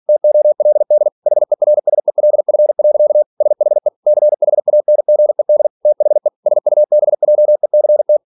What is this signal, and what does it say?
Morse code, heard as a keyed tone, about 46 wpm, 600 Hz. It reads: TOLD HERSELF. SHE CHANGED THE SUBJECT